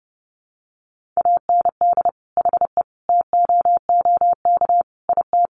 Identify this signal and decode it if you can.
Morse code, heard as a keyed tone, about 30 wpm, 700 Hz. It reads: AND HE TOOK IT
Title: Sentences from Top 1400 Words 30wpm